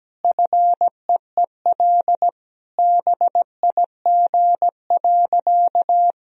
Morse code, heard as a keyed tone, about 17 wpm, 700 Hz. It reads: FEEL BIG.
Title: Sentences from Top 1000 Words - Review Part 1 - 17wpm